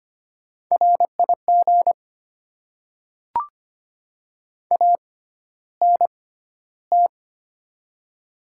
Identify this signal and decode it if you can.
Morse code, heard as a keyed tone, about 25 wpm, spaced as for 13 wpm, 700 Hz. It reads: RIG ANT
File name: Top 20 QSO Elements - 6x Character Spacing - 25wpm